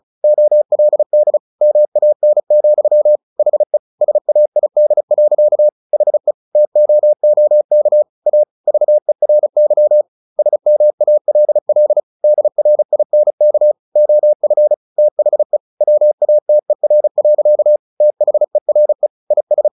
Morse code, heard as a keyed tone, about 34 wpm, 600 Hz. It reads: OLD MAN, HE SAID. HE TOOK A VERY SMALL DRINK OF THE WATER. THERE IS
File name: the old man and the sea 2 35wpm